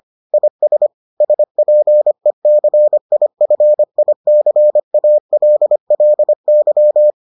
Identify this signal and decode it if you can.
Morse code, heard as a keyed tone, about 25 wpm, 600 Hz. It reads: IS SPECIFICALLY